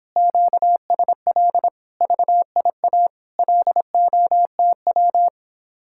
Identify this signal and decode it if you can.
Morse code, heard as a keyed tone, about 31 wpm, 700 Hz. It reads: QSL VIA LOTW